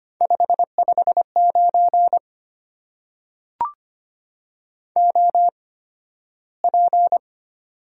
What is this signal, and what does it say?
Morse code, heard as a keyed tone, about 25 wpm, spaced as for 13 wpm, 690 Hz. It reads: 559 OP
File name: Top 90 QSO Elements - 8x Character Spacing - 25wpm